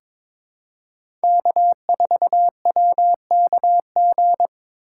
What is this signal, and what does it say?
Morse code, heard as a keyed tone, about 22 wpm, 700 Hz. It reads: K4WKG
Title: USA Call Signs - 6x Character Spacing - 22wpm